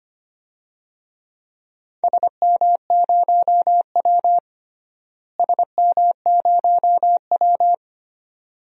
Morse code, heard as a keyed tone, about 25 wpm, 700 Hz. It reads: SM0W SM0W
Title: DXCC-100 Call Signs - 4 Characters - Repeated 2x - 25wpm